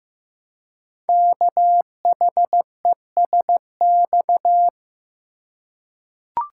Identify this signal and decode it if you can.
Morse code, heard as a keyed tone, about 15 wpm, 700 Hz. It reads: KHESX
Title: Code Groups - 5 Letters - 15wpm